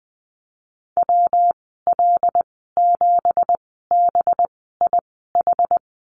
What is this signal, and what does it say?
Morse code, heard as a keyed tone, about 20 wpm, spaced as for 15 wpm, 700 Hz. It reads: WL7BIH
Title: 193 Lesson 31g - Letter v - Call Signs 20wpm